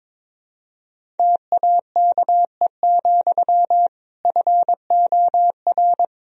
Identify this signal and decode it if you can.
Morse code, heard as a keyed tone, about 22 wpm, 700 Hz. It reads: TAKE, FOR
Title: Sentences from Top 1000 Words - Review Part 1 - 22wpm